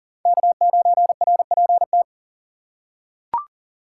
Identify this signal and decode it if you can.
Morse code, heard as a keyed tone, about 37 wpm, 700 Hz. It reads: K9RPT
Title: USA Call Signs - 2x Character Spacing - 40wpm